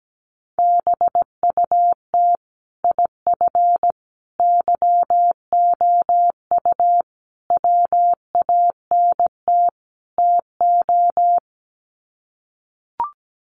Morse code, heard as a keyed tone, about 17 wpm, 700 Hz. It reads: BUT IF YOU WANT TO
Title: Sets of 5 Words Encore2 17wpm